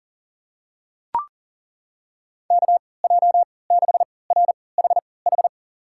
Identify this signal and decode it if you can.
Morse code, heard as a keyed tone, about 40 wpm, spaced as for 23 wpm, 700 Hz. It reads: KJ6RHH
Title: USA Call Signs - 6 Characters - 3x Character Spacing - 40wpm